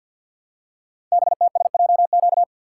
Morse code, heard as a keyed tone, about 50 wpm, 700 Hz. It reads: BTSJQ
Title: Code Groups - 5 Letters - 50wpm